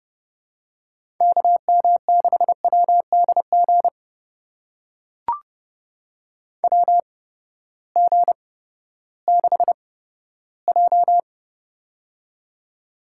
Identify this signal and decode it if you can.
Morse code, heard as a keyed tone, about 30 wpm, spaced as for 15 wpm, 700 Hz. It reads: KM6WDG WG6J
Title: USA Call Signs - 8x Character Spacing - 30wpm